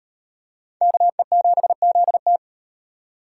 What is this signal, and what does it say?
Morse code, heard as a keyed tone, about 33 wpm, 700 Hz. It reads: KE7ZT